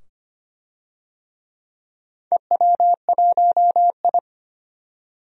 Morse code, heard as a keyed tone, about 25 wpm, 700 Hz. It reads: EW1I
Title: DXCC-100 Call Signs - 4 Characters - 25wpm